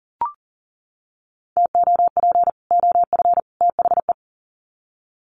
Morse code, heard as a keyed tone, about 40 wpm, 700 Hz. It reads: TOP OF THE
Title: Sets of 3 Words 40wpm